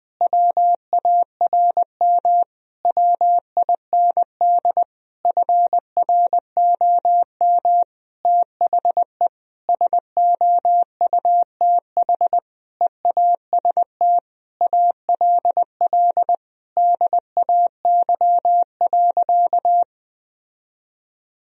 Morse code, heard as a keyed tone, about 20 wpm, 700 Hz. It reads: WARM WIND FROM THE SOUTH EAST ALL DAY.
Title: Sentences from Top 1000 Words - Review Part 3 - 20wpm